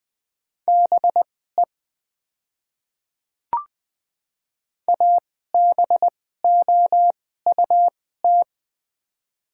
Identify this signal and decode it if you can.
Morse code, heard as a keyed tone, about 20 wpm, spaced as for 15 wpm, 700 Hz. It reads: BE ABOUT
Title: 2x Character Spacing - Top 300 Words - 20wpm